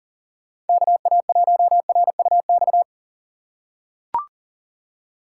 Morse code, heard as a keyed tone, about 40 wpm, 700 Hz. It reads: KA1RUX
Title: USA Call Signs - 6 Characters - 2x Character Spacing - 40wpm